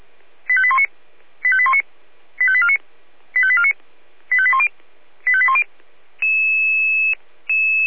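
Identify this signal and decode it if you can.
Five-tone selective call (ZVEI), heard as quick runs of five tones, tones of about 70 ms. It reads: 85819 85819 85849 85849 85810 85810
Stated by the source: Five-tone ZVEI selective call received by SDR.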